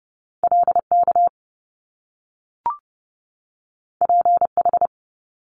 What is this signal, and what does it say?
Morse code, heard as a keyed tone, about 30 wpm, 700 Hz. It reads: LK PH
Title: Top Two Letter Combinations in Words 30wpm